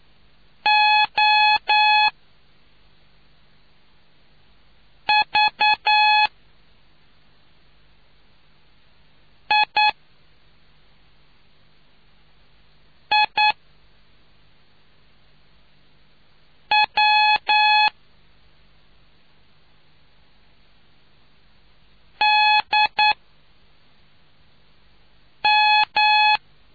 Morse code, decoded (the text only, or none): O V I I W D M